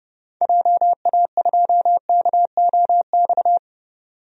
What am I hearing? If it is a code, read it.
Morse code, heard as a keyed tone, about 30 wpm, 700 Hz. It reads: JA2KOX
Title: DXCC-100 Call Signs - 6 Characters - 30wpm